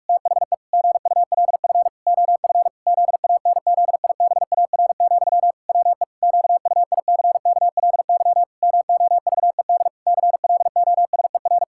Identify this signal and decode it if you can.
Morse code, heard as a keyed tone, about 48 wpm, 700 Hz. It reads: THE GULF OF ZANZIBAR, WE QUICKLY MOVED CLOSER
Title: Pangram Challenge v2 - 45wpm